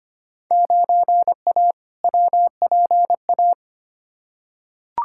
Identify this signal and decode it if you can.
Morse code, heard as a keyed tone, about 25 wpm, 700 Hz. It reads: 9A WPA
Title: Field Day Exchanges Encore - 25wpm